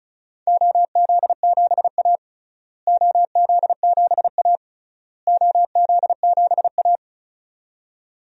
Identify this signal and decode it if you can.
Morse code, heard as a keyed tone, about 36 wpm, 700 Hz. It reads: OZ7A OZ7A OZ7A
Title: DXCC-100 Call Signs - 4 Characters - Repeated 3x - 35wpm